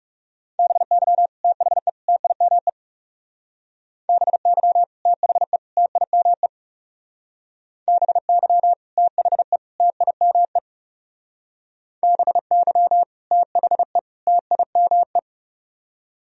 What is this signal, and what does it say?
Morse code, heard as a keyed tone, about 37 wpm, 700 Hz. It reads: BY THE TIME BY THE TIME BY THE TIME BY THE TIME
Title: Speed-Racing - Sets of 3 Words 30wpm